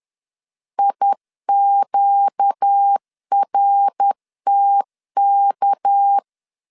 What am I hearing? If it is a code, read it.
Morse code, heard as a keyed tone, about 10 wpm, 800 Hz. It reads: IQRTK
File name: LEZIONI / 6 PKX0 / 6 PK 10wpm 2436